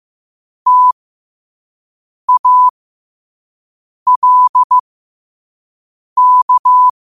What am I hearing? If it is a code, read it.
Morse code, heard as a keyed tone, about 15 wpm, spaced as for 5 wpm, 1000 Hz. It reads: TALK